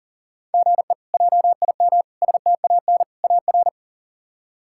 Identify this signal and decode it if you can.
Morse code, heard as a keyed tone, about 40 wpm, 700 Hz. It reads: GE JIM STAN AR
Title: K1USN SST QSO Practice - 40wpm